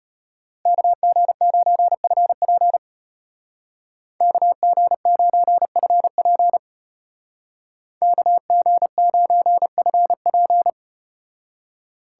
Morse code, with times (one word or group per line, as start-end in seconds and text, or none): 0.7-2.8 KG9FP
4.2-6.6 KG9FP
8.0-10.7 KG9FP